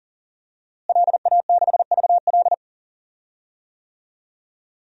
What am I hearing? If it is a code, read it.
Morse code, heard as a keyed tone, about 40 wpm, 700 Hz. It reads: LA6VL
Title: DXCC-100 Call Signs - 5 Characters - Repeated 3x - 40wpm